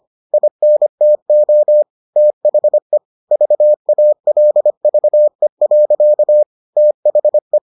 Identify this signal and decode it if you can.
Morse code, heard as a keyed tone, about 25 wpm, 600 Hz. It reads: INTO THE VALVE. THE